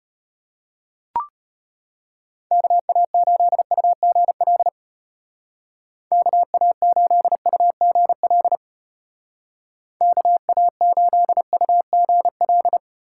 Morse code, heard as a keyed tone, about 31 wpm, 700 Hz. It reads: KA8UGL KA8UGL KA8UGL